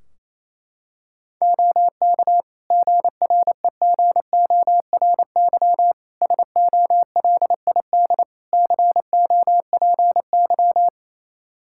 Morse code, heard as a keyed tone, about 28 wpm, 700 Hz. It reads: OK GREGORY SOLID COPY
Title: Practice QSO - Exchange 3 - More Info - 28wpm